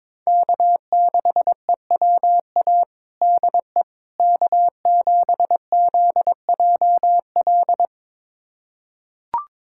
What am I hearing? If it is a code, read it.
Morse code, heard as a keyed tone, about 22 wpm, 700 Hz. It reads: K6EWA DE K7ZJL
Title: USA Callsign DE Callsign - 5 Characters - 22wpm